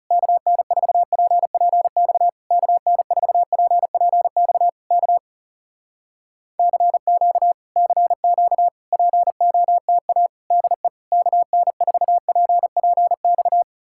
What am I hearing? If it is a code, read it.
Morse code, heard as a keyed tone, about 40 wpm, 700 Hz. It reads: KN4PPX KN4PPX K CQ CQ POTA DE KN4PPX